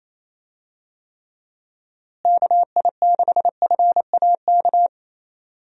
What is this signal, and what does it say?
Morse code, heard as a keyed tone, about 28 wpm, 700 Hz. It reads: KI6FAK
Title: USA Call Signs - 6 Characters - Repeated 3x - 28wpm